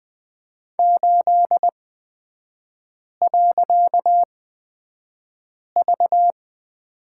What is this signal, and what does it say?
Morse code, heard as a keyed tone, about 20 wpm, spaced as for 5 wpm, 690 Hz. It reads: 8.V
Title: 255 Lesson 41e - Period - Three Characters 20wpm